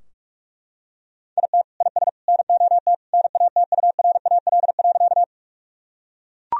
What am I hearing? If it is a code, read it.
Morse code, heard as a keyed tone, about 41 wpm, 700 Hz. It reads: IT IS NOT NATURAL.
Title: Sentences from Top 1000 Words - Review Part 4 - 45wpm